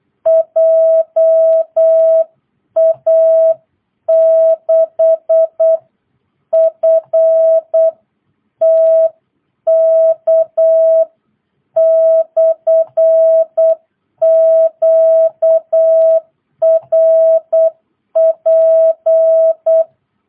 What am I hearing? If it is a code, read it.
Morse code, heard as a keyed tone, about 8 wpm, 660 Hz. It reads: JA6FTK/QRP